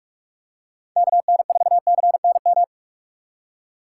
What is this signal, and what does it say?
Morse code, heard as a keyed tone, about 43 wpm, 700 Hz. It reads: KN4CNM
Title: USA Call Signs - 2x Character Spacing - 45wpm